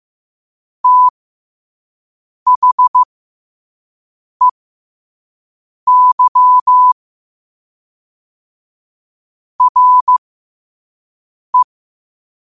Morse code, heard as a keyed tone, about 15 wpm, spaced as for 5 wpm, 1000 Hz. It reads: THEY RE